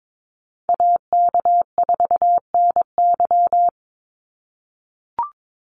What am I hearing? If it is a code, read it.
Morse code, heard as a keyed tone, about 22 wpm, 700 Hz. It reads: AK4NY
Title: Call Signs - 22wpm